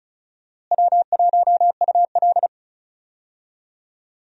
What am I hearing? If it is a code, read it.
Morse code, heard as a keyed tone, about 36 wpm, 700 Hz. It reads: W1UL
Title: USA Call Signs - 4 Characters - Repeated 2x - 35wpm